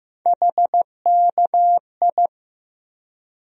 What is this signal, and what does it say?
Morse code, heard as a keyed tone, about 15 wpm, 700 Hz. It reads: HKI
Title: USA Call Sign Postfixes - 3 Letters 15wpm